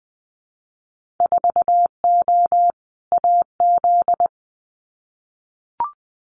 Morse code, heard as a keyed tone, about 20 wpm, 700 Hz. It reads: 4O AZ
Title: Winter Field Day Exchanges Encore 20wpm